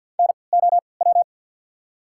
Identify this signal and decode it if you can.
Morse code, heard as a keyed tone, about 53 wpm, spaced as for 29 wpm, 690 Hz. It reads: NOW
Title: 3x Character Spacing - Top 100 Words - 50wpm